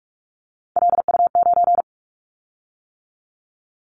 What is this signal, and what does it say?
Morse code, heard as a keyed tone, about 44 wpm, 700 Hz. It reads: LU9
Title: Callsign Prefixes for Top 100 DXCC - 45wpm